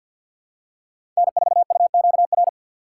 Morse code, heard as a keyed tone, about 51 wpm, 700 Hz. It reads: N4UQR
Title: USA Call Signs - 3x Character Spacing - 50wpm